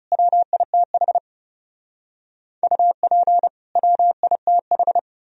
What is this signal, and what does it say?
Morse code, heard as a keyed tone, about 36 wpm, 700 Hz. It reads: WITH UP WITH
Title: Speed-Racing - Sets of 2 Words 30wpm